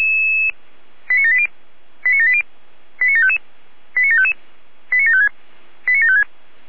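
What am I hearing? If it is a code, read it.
Five-tone selective call (ZVEI), heard as quick runs of five tones, tones of about 70 ms. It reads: 89780 89780 89755 89755 89756 89756
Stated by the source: Received on a software-defined radio (SDR); ZVEI five-tone selective call 89740.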